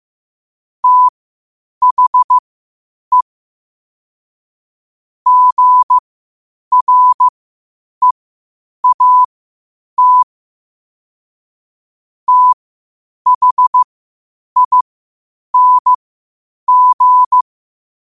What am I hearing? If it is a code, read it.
Morse code, heard as a keyed tone, about 15 wpm, spaced as for 8 wpm, 1000 Hz. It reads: THE GREAT THING
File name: qotdfiles / 8wpm